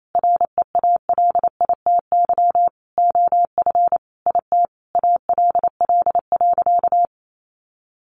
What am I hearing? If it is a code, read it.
Morse code, heard as a keyed tone, about 28 wpm, 700 Hz. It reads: REALITY OF IT ALL.
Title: Sentences from Top 1500 Words 28wpm